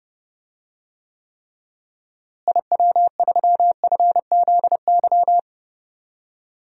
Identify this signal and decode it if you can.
Morse code, heard as a keyed tone, about 30 wpm, 700 Hz. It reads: IW3FZY
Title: DXCC-100 Call Signs - 6 Characters - Repeated 3x - 30wpm